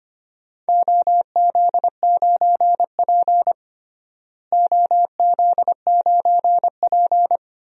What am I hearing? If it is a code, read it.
Morse code, heard as a keyed tone, about 25 wpm, 700 Hz. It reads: OZ9P OZ9P
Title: DXCC-100 Call Signs - 4 Characters - Repeated 3x - 25wpm